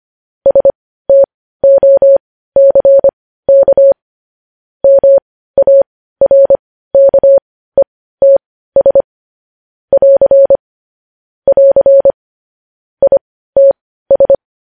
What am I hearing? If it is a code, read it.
Morse code, heard as a keyed tone, about 25 wpm, spaced as for 15 wpm, 550 Hz. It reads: STOCK MARKETS <AR> <AR> ITS
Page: News Headlines in Morse Code at 15 WPM